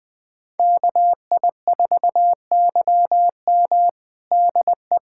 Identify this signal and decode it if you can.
Morse code, heard as a keyed tone, about 21 wpm, 700 Hz. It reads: KI4YM DE
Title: USA Callsign DE Callsign - 5 Characters - 20wpm